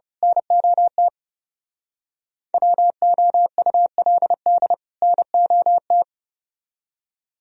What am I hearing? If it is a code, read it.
Morse code, heard as a keyed tone, about 27 wpm, 700 Hz. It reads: NOT WOULD NOT
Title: Speed-Racing - Sets of 2 Words 30wpm